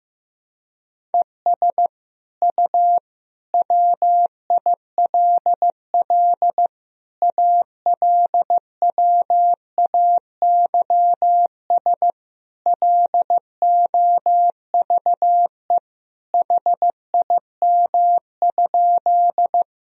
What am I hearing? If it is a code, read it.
Morse code, heard as a keyed tone, about 15 wpm, 700 Hz. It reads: ES U WILL ALWAYS LOVE HIM?
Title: Sentences from Top 400 Words and Abbreviations 15wpm